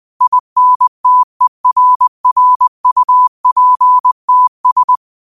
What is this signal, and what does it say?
Morse code, heard as a keyed tone, about 20 wpm, 1000 Hz. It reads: INTERRUPTS
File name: qotdfiles / 20wpm